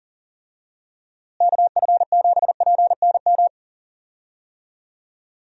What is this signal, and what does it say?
Morse code, heard as a keyed tone, about 37 wpm, 700 Hz. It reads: KF7PNM
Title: USA Call Signs - 6 Characters - Repeated 2x - 40wpm